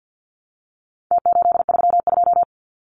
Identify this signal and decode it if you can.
Morse code, heard as a keyed tone, about 49 wpm, 700 Hz. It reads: T832
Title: Numbers - Contest Style - 4-Digit 50wpm